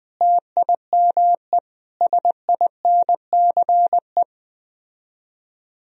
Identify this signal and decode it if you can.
Morse code, heard as a keyed tone, about 21 wpm, 700 Hz. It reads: TIME SINCE